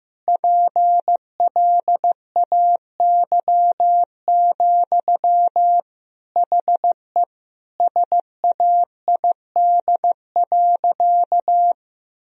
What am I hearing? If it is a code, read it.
Morse code, heard as a keyed tone, about 15 wpm, 700 Hz. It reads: PLAY, HE SAID.
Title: Sentences from Top 1000 Words - Review Part 4 - 15wpm